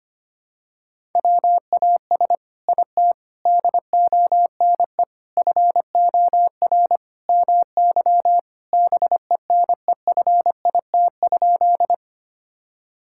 Morse code, heard as a keyed tone, about 25 wpm, 700 Hz. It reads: WAS IT DONE FOR MY BENEFIT?
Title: Sentences from Top 1000 Words - Review Part 6 - 25wpm